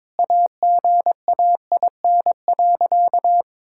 Morse code, heard as a keyed tone, about 22 wpm, 700 Hz. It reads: AGAIN.
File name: practice / 22wpm / Sentences with 4-Word Phrases - 22wpm